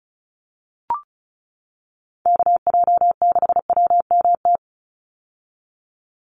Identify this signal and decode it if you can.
Morse code, heard as a keyed tone, about 34 wpm, 700 Hz. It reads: KJ6WMT
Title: USA Call Signs 35wpm